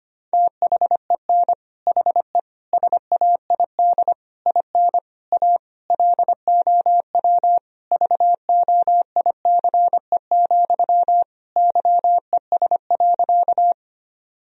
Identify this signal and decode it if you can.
Morse code, heard as a keyed tone, about 25 wpm, 700 Hz. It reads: THEN HE SAID IN A LOW VOICE, YES.
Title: Sentences from Top 1000 Words - Review Part 1 - 25wpm